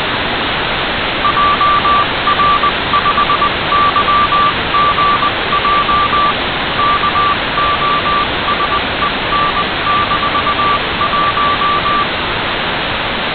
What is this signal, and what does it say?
Morse code, heard as a keyed tone, about 20 wpm, 1170 Hz, with noise. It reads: JR5YGJ KOSEN-1
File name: Kosen-1 CW Beacon received by Igor PU4ELT